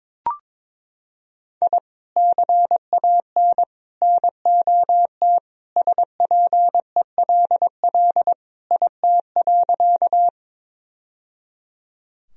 Morse code, heard as a keyed tone, about 22 wpm, 700 Hz. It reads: I CAN NOT SPELL IT.
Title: Sentences from Top 1000 Words - Review Part 1 - 22wpm